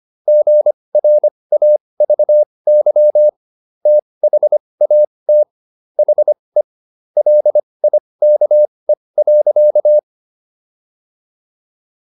Morse code, heard as a keyed tone, about 25 wpm, spaced as for 20 wpm, 600 Hz. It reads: GRAVY THAT HE LIKE.